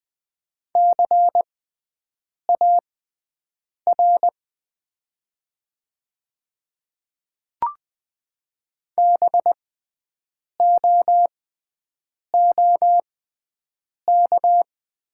Morse code, heard as a keyed tone, about 20 wpm, spaced as for 7 wpm, 700 Hz. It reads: CAR BOOK
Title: 6x Character Spacing - Top 500 Words - 20wpm